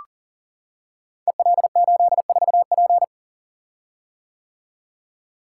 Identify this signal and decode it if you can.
Morse code, heard as a keyed tone, about 39 wpm, 690 Hz. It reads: EL84P
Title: Code Groups - 5 Letter:Numbers - 40wpm